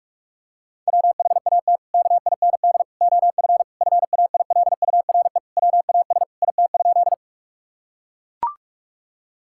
Morse code, heard as a keyed tone, about 42 wpm, 700 Hz. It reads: WHAT KIND OF FAILURE WAS IT?
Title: Sentences from Top 1000 Words - Review Part 6 - 45wpm